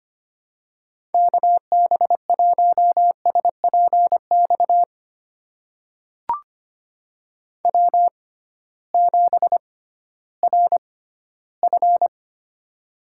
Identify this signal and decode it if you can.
Morse code, heard as a keyed tone, about 25 wpm, spaced as for 15 wpm, 700 Hz. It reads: KB1SPX W7RF
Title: USA Call Signs - 6x Character Spacing - 25wpm